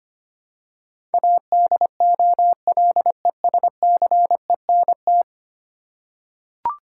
Morse code, heard as a keyed tone, about 25 wpm, 700 Hz. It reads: ADOLESCENT
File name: 10 Letter Words - First 3 Letters Given - 25wpm